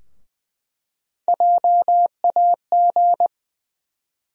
Morse code, heard as a keyed tone, about 20 wpm, 700 Hz. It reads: JAG
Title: USA Call Sign Postfixes - 3 Letters 20wpm